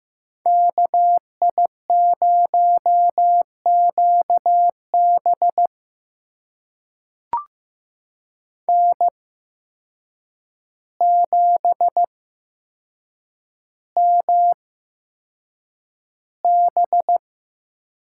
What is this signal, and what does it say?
Morse code, heard as a keyed tone, about 15 wpm, spaced as for 7 wpm, 700 Hz. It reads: KI0QB N7MB
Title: USA Call Signs - 5 Characters - 8x Character Spacing - 15wpm